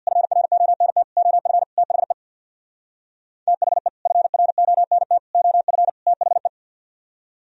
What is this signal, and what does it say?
Morse code, heard as a keyed tone, about 53 wpm, 700 Hz. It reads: FRONT OF THE THE FRONT OF THE
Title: Speed-Racing - Sets of 4 Words 50wpm